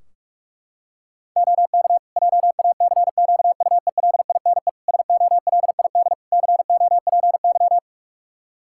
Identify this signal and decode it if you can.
Morse code, heard as a keyed tone, about 46 wpm, 700 Hz. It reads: OK JACQUELINE SOLID COPY
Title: K1USN SST QSO Practice 45wpm